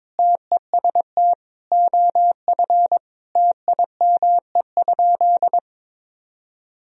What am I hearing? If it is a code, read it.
Morse code, heard as a keyed tone, about 22 wpm, 700 Hz. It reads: TEST OF TIME?
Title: Sentences from Top 1000 Words - Review Part 3 - 22wpm